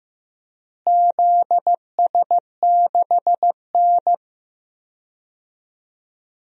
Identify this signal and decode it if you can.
Morse code, heard as a keyed tone, about 15 wpm, 700 Hz. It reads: ZS6N
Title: DXCC-100 Call Signs - 4 Characters - Repeated 3x - 15wpm